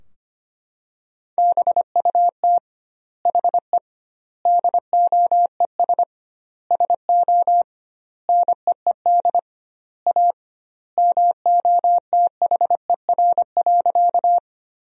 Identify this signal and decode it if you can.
Morse code, heard as a keyed tone, about 25 wpm, 700 Hz. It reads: BUT HE DOES SO NEED A MOTHER.
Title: 2x Word-Spacing - Sentences from Top 200 Words - 25wpm